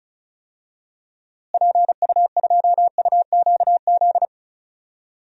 Morse code, heard as a keyed tone, about 34 wpm, 700 Hz. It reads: PU2UQZ
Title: DXCC-100 Call Signs - 6 Characters - Repeated 3x - 35wpm